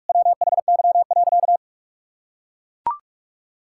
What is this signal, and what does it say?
Morse code, heard as a keyed tone, about 44 wpm, 700 Hz. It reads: WHY.
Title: Sentences from Top 1000 Words - Review Part 6 - 45wpm